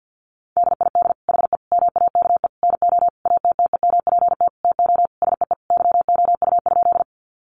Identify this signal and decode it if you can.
Morse code, heard as a keyed tone, about 50 wpm, 700 Hz. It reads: DID HE MAKE NO ATTEMPT TO SEE YOU?
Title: Sentences from Top 1000 Words 50wpm